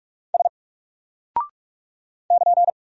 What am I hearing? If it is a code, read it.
Morse code, heard as a keyed tone, about 46 wpm, 700 Hz. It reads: S <KN>
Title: Single Character (Letter:Number:Punctuation:Prosign) 45wpm